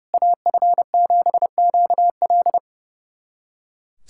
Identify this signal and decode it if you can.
Morse code, heard as a keyed tone, about 32 wpm, 700 Hz. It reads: AF7QL
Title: SOTA QSO Practice 30wpm